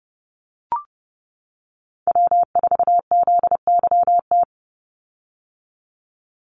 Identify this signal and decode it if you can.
Morse code, heard as a keyed tone, about 30 wpm, 700 Hz. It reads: W4ZYT
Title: Call Signs - 30wpm